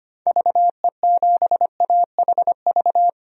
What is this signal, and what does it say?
Morse code, heard as a keyed tone, about 25 wpm, 700 Hz. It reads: VE7AHV
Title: DXCC-100 Call Signs - 6 Characters - Repeated 2x - 25wpm